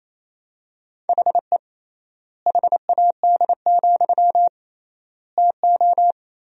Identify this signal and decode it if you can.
Morse code, heard as a keyed tone, about 27 wpm, 700 Hz. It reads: HE HAD, TO
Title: 3x Word-Spacing - Sentences from Top 500 Words - 28wpm